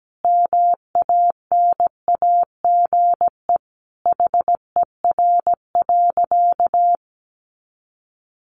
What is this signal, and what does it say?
Morse code, heard as a keyed tone, about 17 wpm, 700 Hz. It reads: MANAGE HER.
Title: Sentences from Top 1400 Words 17wpm